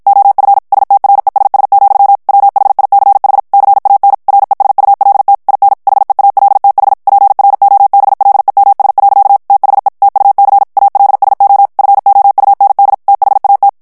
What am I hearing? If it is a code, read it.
Morse code, heard as a keyed tone, about 55 wpm, 800 Hz. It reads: OF STRESS, WHICH CAN RESULT IN HEALTH PROBLEMS. THE TUC ALSO FOUND THAT
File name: morse mp3 / ANGLAIS texte clair 1 vitesse 50 mots minute